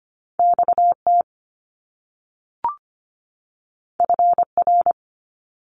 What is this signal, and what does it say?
Morse code, heard as a keyed tone, about 25 wpm, 700 Hz. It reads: XT FR
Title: Top Two Letter Combinations in Words 25wpm